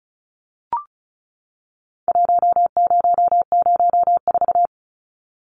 Morse code, heard as a keyed tone, about 34 wpm, 700 Hz. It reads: 1004
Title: Numbers - Contest Style - 4-Digit 35wpm